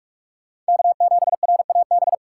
Morse code, heard as a keyed tone, about 58 wpm, 700 Hz. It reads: K7RAB